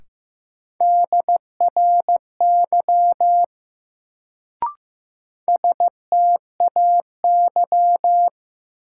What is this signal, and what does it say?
Morse code, heard as a keyed tone, about 15 wpm, 700 Hz. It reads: DRY STAY
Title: Sentences from Top 500 Words - New Word Drill - 15wpm